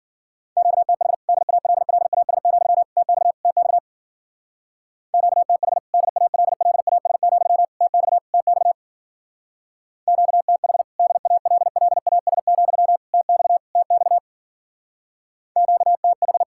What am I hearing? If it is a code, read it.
Morse code, heard as a keyed tone, about 55 wpm, 700 Hz. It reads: QTH DALLAS, TX TX QTH DALLAS, TX TX QTH DALLAS, TX TX QTH